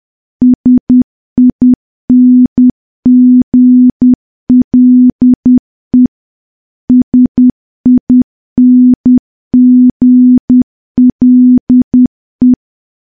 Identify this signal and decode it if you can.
Morse code, heard as a keyed tone, about 10 wpm, 250 Hz. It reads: SINGLE SINGLE
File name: morse / morse-training-10WPM